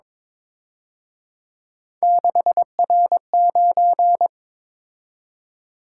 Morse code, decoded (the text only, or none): 6R9